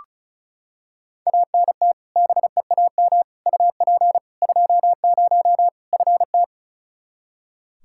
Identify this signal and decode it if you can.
Morse code, heard as a keyed tone, about 34 wpm, 700 Hz. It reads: ANT BEAM UP 20 FT
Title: Practice QSO - Exchange 2 - Info - 35wpm